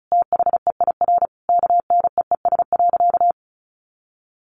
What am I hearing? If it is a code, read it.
Morse code, heard as a keyed tone, about 34 wpm, 700 Hz. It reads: THEIR KNEES.
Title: A Sound Of Thunder 35wpm